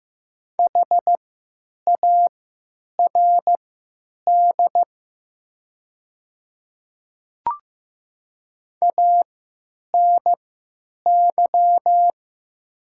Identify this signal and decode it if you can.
Morse code, heard as a keyed tone, about 15 wpm, spaced as for 8 wpm, 700 Hz. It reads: HARD ANY
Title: 3x Character Spacing - Top 300 Words - 15wpm